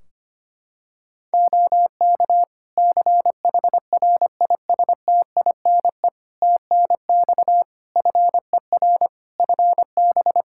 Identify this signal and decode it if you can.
Morse code, heard as a keyed tone, about 25 wpm, 700 Hz. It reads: OK CHRISTINE TNX FER FB
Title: Practice QSO - Exchange 4 - Ending - 25wpm